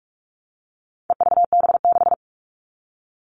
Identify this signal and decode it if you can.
Morse code, heard as a keyed tone, about 44 wpm, 700 Hz. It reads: EVB6
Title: All Cut Numbers - 4-Digit 45wpm